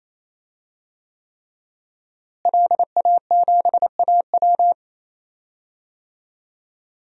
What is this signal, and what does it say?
Morse code, heard as a keyed tone, about 28 wpm, 700 Hz. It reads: LA7AW
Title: DXCC-100 Call Signs - 5 Characters - Repeated 3x - 28wpm